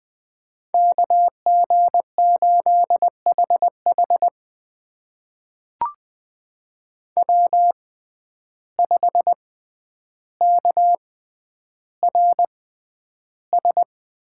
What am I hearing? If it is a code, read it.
Morse code, heard as a keyed tone, about 20 wpm, spaced as for 10 wpm, 700 Hz. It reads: KG8HH W5KRS
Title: USA Call Signs - 5 Characters - 6x Character Spacing - 20wpm